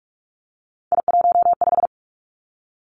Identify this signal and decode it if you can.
Morse code, heard as a keyed tone, about 46 wpm, 700 Hz. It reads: I15
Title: Three Challenging Letter-Number Combinations 45wpm